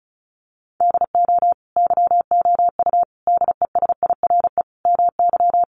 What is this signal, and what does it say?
Morse code, heard as a keyed tone, about 34 wpm, 700 Hz. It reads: DO YOU DESIRE MY
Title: Sentences from Top 1100 Words 35wpm